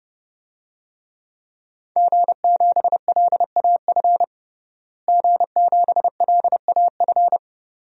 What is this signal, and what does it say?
Morse code, heard as a keyed tone, about 32 wpm, 700 Hz. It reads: G7LAF G7LAF
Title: DXCC-100 Call Signs - 5 Characters - Repeated 2x - 30wpm